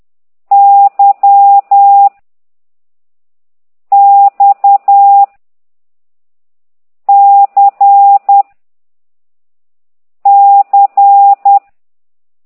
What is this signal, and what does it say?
Morse code, heard as a keyed tone, about 10 wpm, 800 Hz. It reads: Y X C C
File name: ct2idw / exerc15